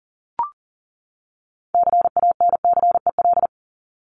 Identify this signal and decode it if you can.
Morse code, heard as a keyed tone, about 40 wpm, 700 Hz. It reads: CANCEL
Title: Top 1001-1250 Words 40wpm